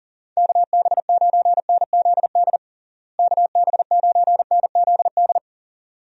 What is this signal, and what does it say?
Morse code, heard as a keyed tone, about 37 wpm, 700 Hz. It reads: KB9NZD KB9NZD
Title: USA Call Signs - 6 Characters - Repeated 2x - 40wpm